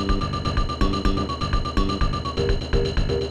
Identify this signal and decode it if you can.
There is no code.